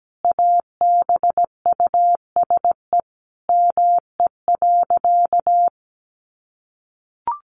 Morse code, heard as a keyed tone, about 17 wpm, 700 Hz. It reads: ABUSE ME.
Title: Sentences from Top 1300 Words 17wpm